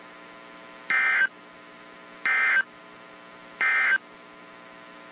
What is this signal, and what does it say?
Emergency Alert System (SAME) tones, heard as harsh buzzing bursts. It is heard from 0.8 to 4.1 s.
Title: WZ2523 - NOAA Weather Radio